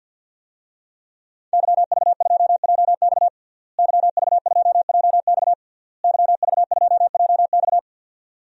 Morse code, heard as a keyed tone, about 52 wpm, 700 Hz. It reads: YV2JX YV2JX YV2JX
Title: DXCC-100 Call Signs - 5 Characters - Repeated 3x - 50wpm